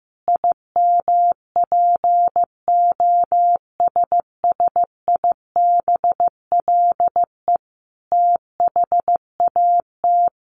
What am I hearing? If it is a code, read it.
Morse code, heard as a keyed tone, about 15 wpm, 700 Hz. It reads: IMPOSSIBLE THAT
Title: Sentences from Top 1100 Words 15wpm